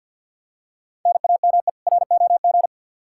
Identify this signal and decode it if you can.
Morse code, heard as a keyed tone, about 52 wpm, 700 Hz. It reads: NAME ROG
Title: Practice QSO - Exchange 1 - Beginning - 50wpm